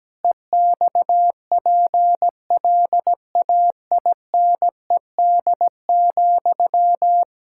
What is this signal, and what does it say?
Morse code, heard as a keyed tone, about 17 wpm, 700 Hz. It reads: EXPLAINED,
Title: Pangram Challenge v2 - 17wpm